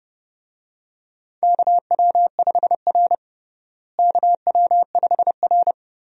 Morse code, heard as a keyed tone, about 30 wpm, 700 Hz. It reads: KW5R KW5R
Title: USA Call Signs - 4 Characters - Repeated 3x - 30wpm